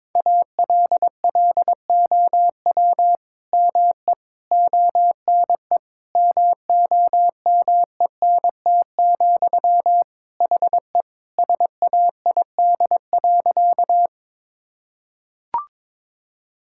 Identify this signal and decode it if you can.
Morse code, heard as a keyed tone, about 22 wpm, 700 Hz. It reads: ALLOW ME ONE MOMENT, HE SAID.
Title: Sentences from Top 1000 Words - Review Part 5 - 22wpm